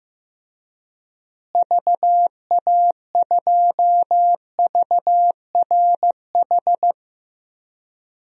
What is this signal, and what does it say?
Morse code, heard as a keyed tone, about 15 wpm, 700 Hz. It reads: VA2VRH
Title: DXCC-100 Call Signs - 6 Characters - Repeated 3x - 15wpm